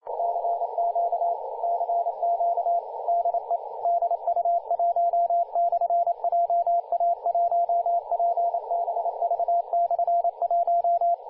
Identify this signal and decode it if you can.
Morse code, heard as a keyed tone, about 29 wpm, 690 Hz, with radio noise. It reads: CQ CQ DE DU1/JA1PBV/1
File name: du1-ja1pbv-1-28-040mhz-cw-cqing